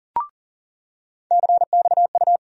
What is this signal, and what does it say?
Morse code, heard as a keyed tone, about 40 wpm, 700 Hz. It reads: CXU
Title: USA Call Sign Postfixes - 3 Letters 40wpm